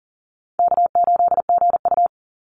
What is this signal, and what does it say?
Morse code, heard as a keyed tone, about 40 wpm, 700 Hz. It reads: K8GU
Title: Call Signs - 40wpm